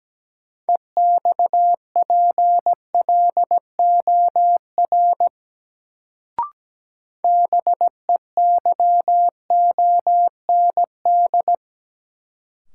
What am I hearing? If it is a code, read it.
Morse code, heard as a keyed tone, about 17 wpm, 700 Hz. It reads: EXPLOR BEYOND
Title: Sentences from Top 1100 Words - New Word Drill Alt - 17wpm